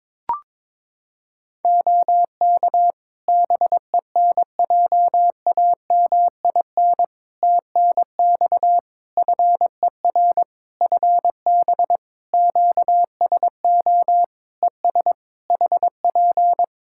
Morse code, heard as a keyed tone, about 22 wpm, 700 Hz. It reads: OK BENJAMIN TNX FER FB QSO ES HP
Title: Practice QSO - Exchange 4 - Ending - 22wpm